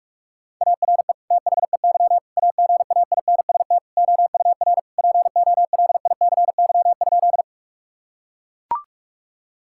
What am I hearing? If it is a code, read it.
Morse code, heard as a keyed tone, about 41 wpm, 700 Hz. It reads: ARE THEY AGAINST OUR POLICY?
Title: Sentences from Top 1000 Words - Review Part 5 - 45wpm